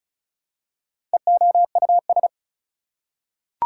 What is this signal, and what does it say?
Morse code, heard as a keyed tone, about 34 wpm, 700 Hz. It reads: EOUS
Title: Prefixes and Suffixes 35wpm